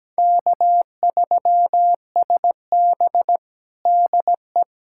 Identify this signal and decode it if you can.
Morse code, heard as a keyed tone, about 17 wpm, 700 Hz. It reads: K3SB DE
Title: USA Callsign DE Callsign - 4 Characters - 17wpm